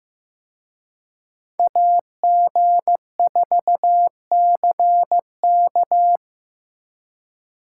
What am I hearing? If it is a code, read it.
Morse code, heard as a keyed tone, about 18 wpm, 700 Hz. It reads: AG4CK
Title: Speed Racing - USA 2x2 Call Signs 15wpm